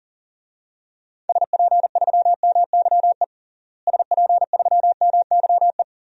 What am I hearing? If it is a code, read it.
Morse code, heard as a keyed tone, about 40 wpm, 700 Hz. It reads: SP3MYE SP3MYE
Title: DXCC-100 Call Signs - 6 Characters - Repeated 3x - 40wpm